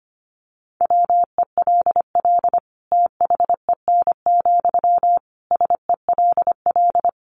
Morse code, heard as a keyed tone, about 25 wpm, 700 Hz. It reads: WELL THEN, SELL
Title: Sentences from Top 1100 Words 25wpm